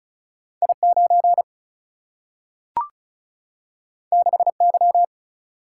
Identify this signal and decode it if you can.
Morse code, heard as a keyed tone, about 36 wpm, 700 Hz. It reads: I9 6Y
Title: Single-Letter Single-Number 35wpm